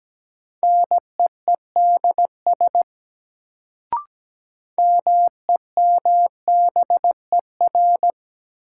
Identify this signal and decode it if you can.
Morse code, heard as a keyed tone, about 17 wpm, 700 Hz. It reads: NEEDS MEMBER
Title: Sets of 3 Words - New Word Drill - 17wpm